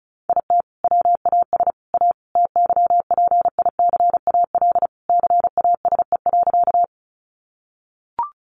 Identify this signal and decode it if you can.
Morse code, heard as a keyed tone, about 34 wpm, 700 Hz. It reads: IT WAS A TYPICAL CASE.
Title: Sentences from Top 1100 Words 35wpm